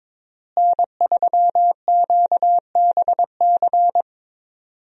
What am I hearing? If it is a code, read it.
Morse code, heard as a keyed tone, about 21 wpm, 700 Hz. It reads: N3QBC
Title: USA Call Signs - 5 Characters - Repeated 2x - 22wpm